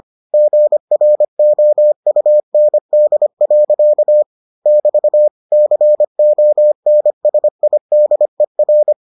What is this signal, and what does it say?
Morse code, heard as a keyed tone, about 25 wpm, 600 Hz. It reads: GROUND. <BT> CONSIDER